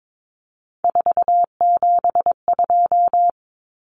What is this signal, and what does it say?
Morse code, heard as a keyed tone, about 22 wpm, 700 Hz. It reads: 472
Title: Numbers - 3-Digit 22wpm